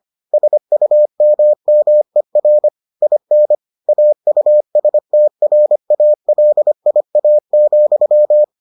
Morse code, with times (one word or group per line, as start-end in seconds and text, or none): 0.3-2.7 SUMMER
3.0-3.5 IN
3.9-8.4 AUSTRALIA,